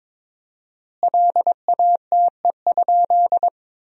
Morse code, heard as a keyed tone, about 22 wpm, 700 Hz. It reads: LATE?
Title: 4x Word-Spacing - Sentences from Top 300 Words - 22wpm